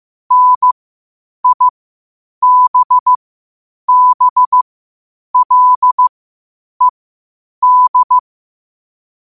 Morse code, heard as a keyed tone, about 15 wpm, spaced as for 8 wpm, 1000 Hz. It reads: NIBBLED
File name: qotdfiles / 8wpm